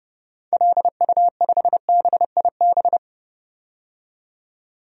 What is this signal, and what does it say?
Morse code, heard as a keyed tone, about 30 wpm, 700 Hz. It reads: LU5BIB
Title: DXCC-100 Call Signs - 6 Characters - 30wpm